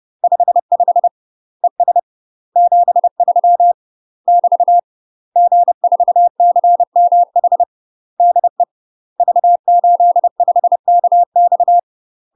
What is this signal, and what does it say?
Morse code, heard as a keyed tone, about 30 wpm, 700 Hz. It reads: 55 ES 73 <BT> G4CMH DE V85KX
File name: UK MASTER QSOs TRAINING / 30WPM / UK MASTER QSOs 0023